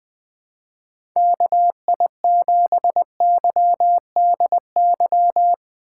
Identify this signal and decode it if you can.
Morse code, heard as a keyed tone, about 21 wpm, 700 Hz. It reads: KI7YDY
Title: USA Call Signs - 6 Characters - 4x Character Spacing - 20wpm